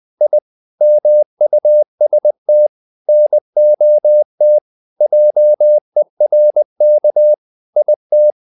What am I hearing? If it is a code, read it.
Morse code, heard as a keyed tone, about 20 wpm, 600 Hz. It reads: I MUST NOT JERK IT
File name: the old man and the sea / the old man and the sea part 1 20wpm